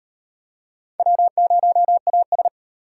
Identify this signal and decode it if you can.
Morse code, heard as a keyed tone, about 38 wpm, 700 Hz. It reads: W0AS